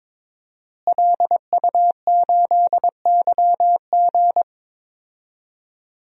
Morse code, heard as a keyed tone, about 21 wpm, 700 Hz. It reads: LU8YG
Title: DXCC-100 Call Signs - 5 Characters - Repeated 3x - 22wpm